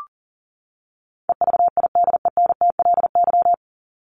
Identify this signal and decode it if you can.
Morse code, heard as a keyed tone, about 40 wpm, 700 Hz. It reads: EVIDENTLY
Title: LY Suffix - Top 100 Words with Base Words 40wpm